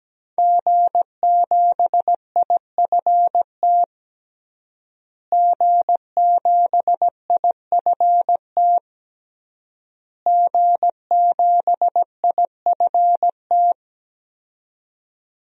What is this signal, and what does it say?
Morse code, heard as a keyed tone, about 17 wpm, 700 Hz. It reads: G7IFT G7IFT G7IFT
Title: DXCC-100 Call Signs - 5 Characters - Repeated 3x - 17wpm